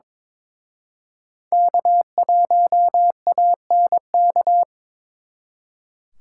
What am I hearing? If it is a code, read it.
Morse code, heard as a keyed tone, about 26 wpm, 700 Hz. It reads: K1ANK
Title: Speed Racing - USA 1x3 Call Signs 22wpm